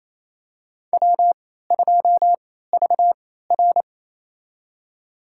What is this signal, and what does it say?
Morse code, heard as a keyed tone, about 28 wpm, spaced as for 16 wpm, 700 Hz. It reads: W2VR
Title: USA Call Signs - 4 Characters - 3x Character Spacing - 28wpm